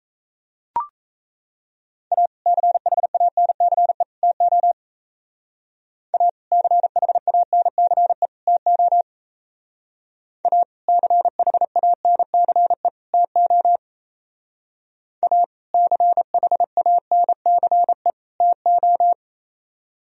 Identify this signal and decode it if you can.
Morse code, heard as a keyed tone, about 35 wpm, 700 Hz. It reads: A CHANCE TO A CHANCE TO A CHANCE TO A CHANCE TO
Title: Speed-Racing - Sets of 3 Words 28wpm